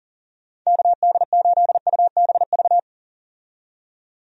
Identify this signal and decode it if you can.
Morse code, heard as a keyed tone, about 40 wpm, 700 Hz. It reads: KD8UBV
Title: USA Call Signs - 6 Characters - 40wpm